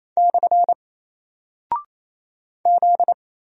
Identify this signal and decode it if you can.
Morse code, heard as a keyed tone, about 27 wpm, 700 Hz. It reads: / Z
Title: Single Character (Letter:Number:Punctuation:Prosign) 28wpm